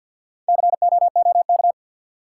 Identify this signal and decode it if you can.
Morse code, heard as a keyed tone, about 51 wpm, 700 Hz. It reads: COOK
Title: Sentences from Top 700 Words - New Word Drill - 50wpm